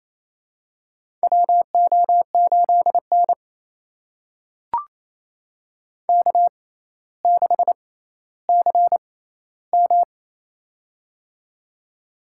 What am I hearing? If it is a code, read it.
Morse code, heard as a keyed tone, about 27 wpm, spaced as for 14 wpm, 700 Hz. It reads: WO8N K6CM
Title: USA Call Signs - 4 Characters - 6x Character Spacing - 28wpm